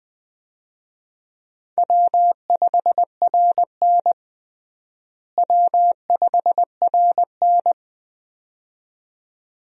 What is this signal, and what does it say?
Morse code, heard as a keyed tone, about 20 wpm, 700 Hz. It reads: W5RN W5RN
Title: USA Call Signs - 4 Characters - Repeated 2x - 20wpm